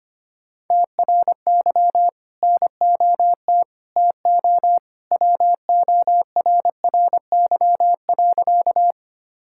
Morse code, heard as a keyed tone, about 25 wpm, 700 Hz. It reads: TRY NOT TO WORRY.
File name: practice / 25wpm / Sentences with 2-Word Phrases - 25wpm